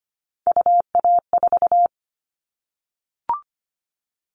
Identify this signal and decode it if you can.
Morse code, heard as a keyed tone, about 25 wpm, 700 Hz. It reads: UA4
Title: Callsign Prefixes for Top 100 DXCC - 25wpm